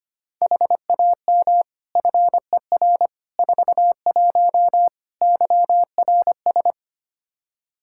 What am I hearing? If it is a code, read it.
Morse code, heard as a keyed tone, about 25 wpm, 700 Hz. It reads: HAM FER 41 YRS
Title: Practice QSO - Exchange 3 - More Info - 25wpm